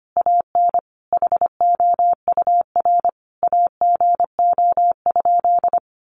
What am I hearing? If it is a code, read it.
Morse code, heard as a keyed tone, about 25 wpm, 700 Hz. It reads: AN HOUR AGO?
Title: Sentences from Top 1300 Words 25wpm